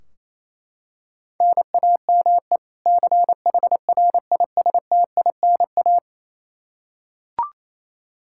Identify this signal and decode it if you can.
Morse code, heard as a keyed tone, about 27 wpm, 700 Hz. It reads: NAME CHRISTINA
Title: Practice QSO - Exchange 1 - Beginning - 28wpm